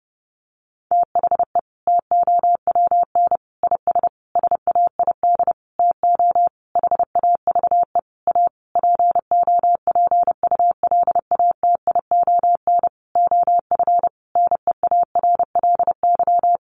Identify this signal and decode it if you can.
Morse code, heard as a keyed tone, about 30 wpm, 700 Hz. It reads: THE TOWN IS SAID TO HAVE A POPULATION OF NEARLY
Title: Sentences from Top 1300 Words 30wpm